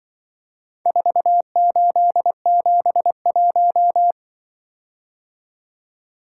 Morse code, heard as a keyed tone, about 24 wpm, 700 Hz. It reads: 4871
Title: Speed-Racing - Four Letter-Number Combinations - 20wpm